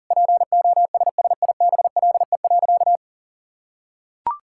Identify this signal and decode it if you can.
Morse code, heard as a keyed tone, about 40 wpm, 700 Hz. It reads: POSSIBLE.
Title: Sentences from Top 1000 Words - Review Part 5 - 40wpm